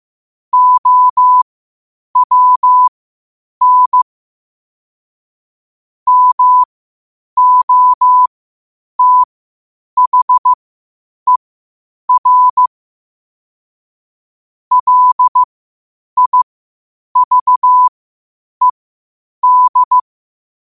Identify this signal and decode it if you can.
Morse code, heard as a keyed tone, about 15 wpm, spaced as for 8 wpm, 1000 Hz. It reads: OWN MOTHER LIVED
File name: qotdfiles / 8wpm